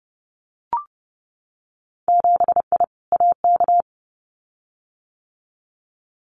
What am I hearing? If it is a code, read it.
Morse code, heard as a keyed tone, about 30 wpm, 700 Hz. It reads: 7I AK
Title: Winter Field Day Exchanges Encore 30wpm